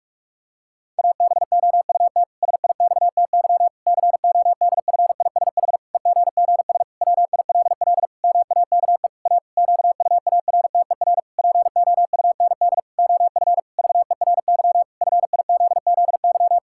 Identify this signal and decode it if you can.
Morse code, heard as a keyed tone, about 46 wpm, 700 Hz. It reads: ABOUT SIXTY CODFISH EGGS WILL MAKE A QUARTER POUND OF VERY FIZZY
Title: Pangram Challenge v2 - 45wpm